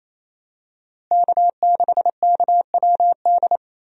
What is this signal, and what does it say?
Morse code, heard as a keyed tone, about 27 wpm, 700 Hz. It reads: K6KWD
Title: USA Call Signs - 5 Characters - Repeated 3x - 28wpm